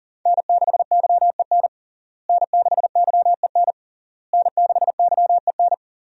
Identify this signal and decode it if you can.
Morse code, heard as a keyed tone, about 40 wpm, 700 Hz. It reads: N6YEN N6YEN N6YEN
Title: USA Call Signs - 5 Characters - Repeated 3x - 40wpm